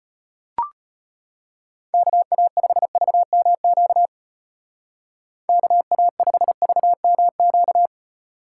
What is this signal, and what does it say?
Morse code, heard as a keyed tone, about 34 wpm, 700 Hz. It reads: KA5VMQ KA5VMQ